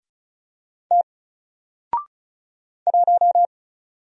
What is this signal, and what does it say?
Morse code, heard as a keyed tone, about 35 wpm, 700 Hz. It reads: T 1
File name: Single-Letter-Number-35wpm